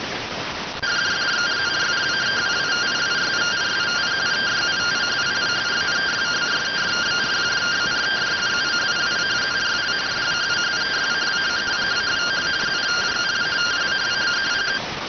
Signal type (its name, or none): two-tone teleprinter signal